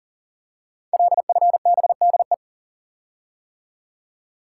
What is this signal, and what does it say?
Morse code, heard as a keyed tone, about 38 wpm, 690 Hz. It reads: LFBDE
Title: Code Groups - 5 Letters - 40wpm